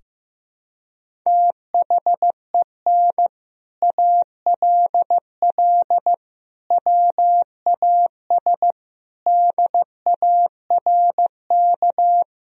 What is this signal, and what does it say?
Morse code, heard as a keyed tone, about 15 wpm, 700 Hz. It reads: THEN ALL WAS DARK
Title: Sentences from Top 500 Words and Abbreviations 15wpm